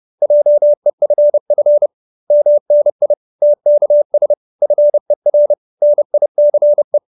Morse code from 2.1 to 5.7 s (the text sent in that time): MNI TKS FER